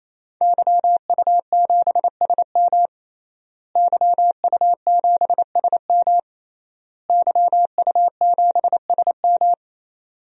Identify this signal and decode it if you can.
Morse code, heard as a keyed tone, about 27 wpm, 700 Hz. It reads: YU7SM YU7SM YU7SM
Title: DXCC-100 Call Signs - 5 Characters - Repeated 3x - 28wpm